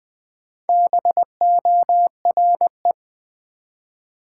Morse code, heard as a keyed tone, about 20 wpm, 690 Hz. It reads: BORE
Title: Build A Word - 1 Letter At A Time - 5-Letter Words - 20wpm